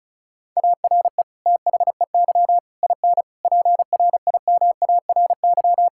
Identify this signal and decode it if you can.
Morse code, heard as a keyed tone, about 36 wpm, 700 Hz. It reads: ARE THEY IN PRIMARY
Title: Sentences from Top 1000 Words - Review Part 5 - 35wpm